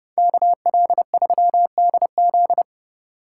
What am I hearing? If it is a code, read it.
Morse code, heard as a keyed tone, about 39 wpm, 700 Hz. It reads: KL3DZ